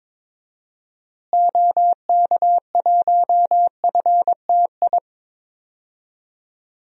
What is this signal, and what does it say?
Morse code, heard as a keyed tone, about 22 wpm, 700 Hz. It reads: OK1FTI
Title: DXCC-100 Call Signs - 6 Characters - Repeated 2x - 22wpm